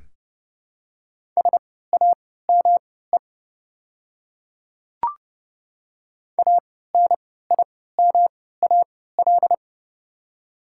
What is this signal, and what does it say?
Morse code, heard as a keyed tone, about 29 wpm, spaced as for 17 wpm, 700 Hz. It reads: SAME ANIMAL
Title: 3x Character Spacing - Top 200 Words - 30wpm